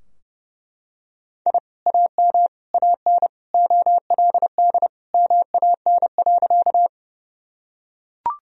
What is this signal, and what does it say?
Morse code, heard as a keyed tone, about 30 wpm, 700 Hz. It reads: I AM AN OLD MAN.
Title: Sentences from Top 1000 Words - Review Part 2 - 30wpm